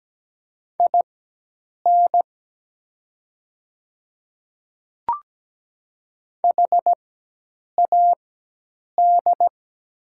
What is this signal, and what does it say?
Morse code, heard as a keyed tone, about 17 wpm, spaced as for 8 wpm, 700 Hz. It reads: IN HAD
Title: 4x Character Spacing - Top 200 Words - 17wpm